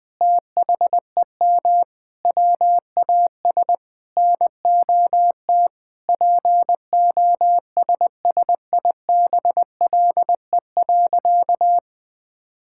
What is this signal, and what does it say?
Morse code, heard as a keyed tone, about 20 wpm, 700 Hz. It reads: THEM WAS NOT POSSIBLE.
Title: Sentences from Top 1000 Words - Review Part 6 - 20wpm